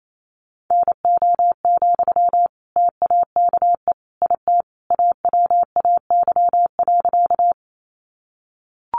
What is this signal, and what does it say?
Morse code, heard as a keyed tone, about 28 wpm, 700 Hz. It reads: NO, TAKE IT AWAY.
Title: Sentences from Top 600 Words 28wpm